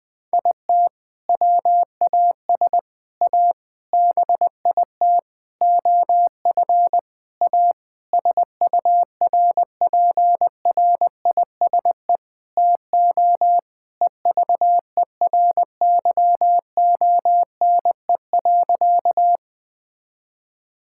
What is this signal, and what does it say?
Morse code, heard as a keyed tone, about 20 wpm, 700 Hz. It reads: IT WAS A BIT OF A SURPRISE TO EVERYONE.
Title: Sentences with 6-Word Phrases - 20wpm